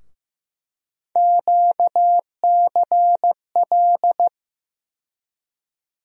Morse code, heard as a keyed tone, about 15 wpm, 700 Hz. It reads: QCL
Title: USA Call Sign Postfixes - 3 Letters 15wpm